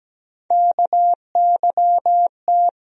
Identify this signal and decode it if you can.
Morse code, heard as a keyed tone, about 17 wpm, 700 Hz. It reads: KYT
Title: USA Call Sign Postfixes - 3 Letters 17wpm